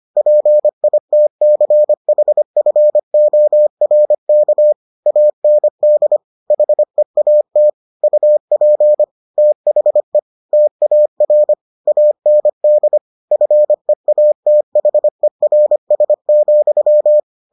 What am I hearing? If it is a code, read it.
Morse code, heard as a keyed tone, about 25 wpm, 600 Hz. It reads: PITCHFORK AND HEAT UP THE TAR AND FEATHERS,